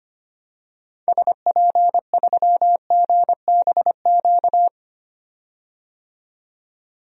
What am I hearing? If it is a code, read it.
Morse code, heard as a keyed tone, about 25 wpm, 700 Hz. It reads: SP3GBQ
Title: DXCC-100 Call Signs - 6 Characters - 25wpm